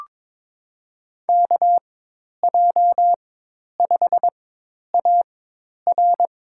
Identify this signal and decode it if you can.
Morse code, heard as a keyed tone, about 22 wpm, spaced as for 10 wpm, 700 Hz. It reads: KJ5AR
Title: USA Call Signs - 5 Characters - 4x Character Spacing - 22wpm